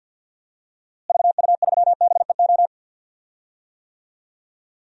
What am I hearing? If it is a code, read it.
Morse code, heard as a keyed tone, about 45 wpm, 700 Hz. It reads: VU3BEO
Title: DXCC-100 Call Signs - 6 Characters - Repeated 2x - 50wpm